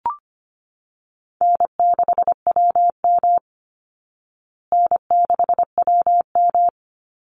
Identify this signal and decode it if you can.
Morse code, heard as a keyed tone, about 25 wpm, 700 Hz. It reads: N6WM N6WM
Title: Call Signs Repeated 2x - 25wpm